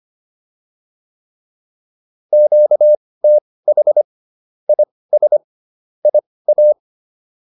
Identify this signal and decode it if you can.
Morse code, heard as a keyed tone, about 25 wpm, spaced as for 18 wpm, 600 Hz. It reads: QTH IS IA